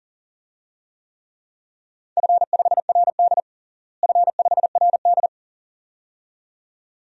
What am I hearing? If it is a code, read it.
Morse code, heard as a keyed tone, about 40 wpm, 700 Hz. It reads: F5RD F5RD
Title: DXCC-100 Call Signs - 4 Characters - Repeated 2x - 40wpm